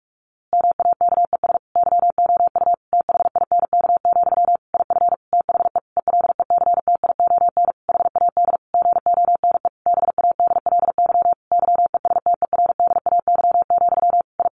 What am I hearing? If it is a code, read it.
Morse code, heard as a keyed tone, about 44 wpm, 700 Hz. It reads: MAKES YOU THINK, IF THE ELECTION HAD GONE BADLY YESTERDAY, I
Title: A Sound Of Thunder 45wpm